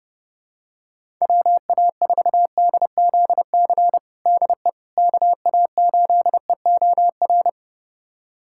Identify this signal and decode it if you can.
Morse code, heard as a keyed tone, about 30 wpm, 700 Hz. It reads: WA4DZC DE KA8EOR
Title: USA Callsign DE Callsign - 6 Characters - 30wpm